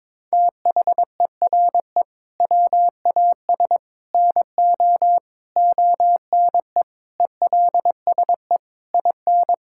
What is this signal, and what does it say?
Morse code, heard as a keyed tone, about 22 wpm, 700 Hz. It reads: THERE WAS NO ONE ELSE IN
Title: Sentences from Top 1600 Words 22wpm